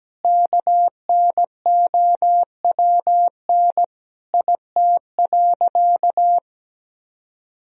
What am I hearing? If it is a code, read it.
Morse code, heard as a keyed tone, about 17 wpm, 700 Hz. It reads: KNOWN IT.
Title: Sentences from Top 1000 Words - Review Part 3 - 17wpm